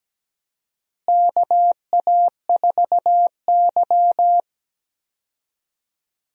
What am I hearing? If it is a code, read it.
Morse code, heard as a keyed tone, about 17 wpm, 700 Hz. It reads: KA4Y
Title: USA Call Signs - 4 Characters - Repeated 2x - 17wpm